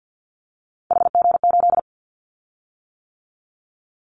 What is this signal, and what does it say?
Morse code, heard as a keyed tone, about 50 wpm, 700 Hz. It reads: HG8
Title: Callsign Prefixes for Top 25 DXCC - 50wpm